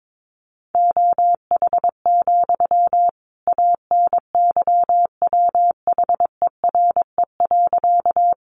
Morse code, heard as a keyed tone, about 22 wpm, 700 Hz. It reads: OH, ANYWHERE.
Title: Sentences from Top 1400 Words 22wpm